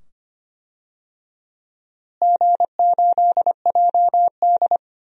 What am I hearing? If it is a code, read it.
Morse code, heard as a keyed tone, about 25 wpm, 700 Hz. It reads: G8JD
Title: DXCC-100 Call Signs - 4 Characters - Repeated 2x - 25wpm